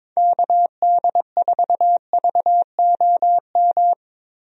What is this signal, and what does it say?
Morse code, heard as a keyed tone, about 22 wpm, 700 Hz. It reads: KD4VOM
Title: USA Call Signs - 6 Characters - Repeated 3x - 22wpm